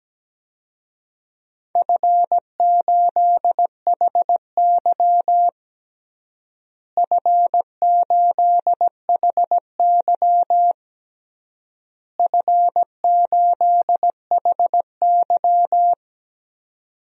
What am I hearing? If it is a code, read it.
Morse code, heard as a keyed tone, about 17 wpm, 700 Hz. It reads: F8HY F8HY F8HY
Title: DXCC-100 Call Signs - 4 Characters - Repeated 3x - 17wpm